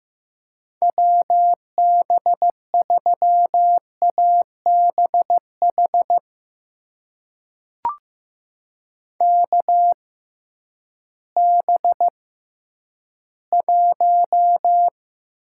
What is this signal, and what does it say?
Morse code, heard as a keyed tone, about 16 wpm, spaced as for 10 wpm, 700 Hz. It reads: WB3ABH KB1
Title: USA Call Signs - 6 Characters - 6x Character Spacing - 15wpm